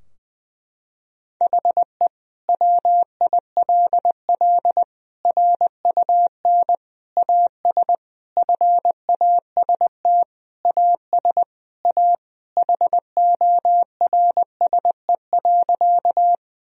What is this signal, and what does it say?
Morse code, heard as a keyed tone, about 20 wpm, 700 Hz. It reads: HE WILL RUN AS FAST AS A HORSE.
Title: Sentences from Top 1000 Words - Review Part 2 - 20wpm